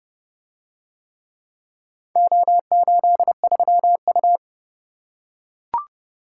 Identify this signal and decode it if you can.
Morse code, heard as a keyed tone, about 30 wpm, 700 Hz. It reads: O83U